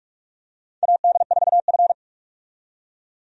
Morse code, heard as a keyed tone, about 46 wpm, 700 Hz. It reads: AD4F
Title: USA Call Signs - 4 Characters - 45wpm